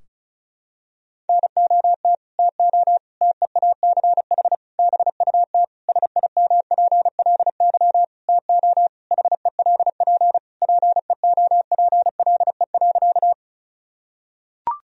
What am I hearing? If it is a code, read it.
Morse code, heard as a keyed tone, about 36 wpm, 700 Hz. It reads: NOT TO TEACH BUT SIMPLY TO HELP PEOPLE.
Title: Sentences from Top 1000 Words - Review Part 4 - 35wpm